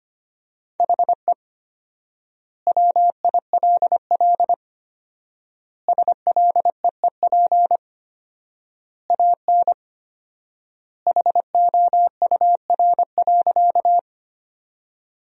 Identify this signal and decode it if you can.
Morse code, heard as a keyed tone, about 25 wpm, 700 Hz. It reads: HE WILL SLEEP AN HOUR.
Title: 4x Word-Spacing - Sentences from Top 500 Words - 25wpm